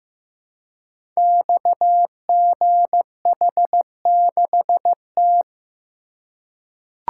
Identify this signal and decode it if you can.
Morse code, heard as a keyed tone, about 15 wpm, 700 Hz. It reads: XGH6T
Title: Code Groups - 5 Letter:Numbers - 15wpm